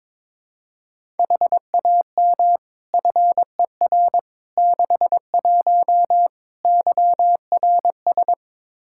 Morse code, heard as a keyed tone, about 22 wpm, 700 Hz. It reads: HAM FER 61 YRS
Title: K1USN SST QSO Practice 22wpm